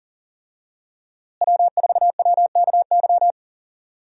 Morse code, heard as a keyed tone, about 40 wpm, 700 Hz. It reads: W4WKY
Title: USA Call Signs - 5 Characters - Repeated 2x - 40wpm